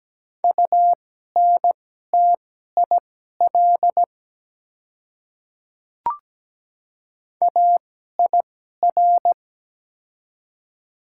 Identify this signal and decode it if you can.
Morse code, heard as a keyed tone, about 17 wpm, spaced as for 12 wpm, 700 Hz. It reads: UNTIL AIR
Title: 2x Character Spacing - Top 300 Words - 17wpm